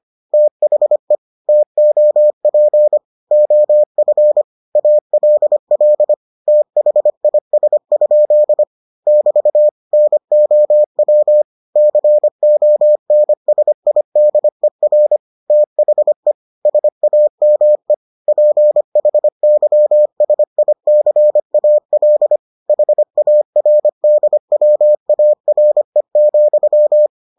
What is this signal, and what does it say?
Morse code, heard as a keyed tone, about 25 wpm, 600 Hz. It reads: THE TOP OF ALL THIS? <BT> NOW CONSIDER THE SAME PHYSICAL HARDWARE,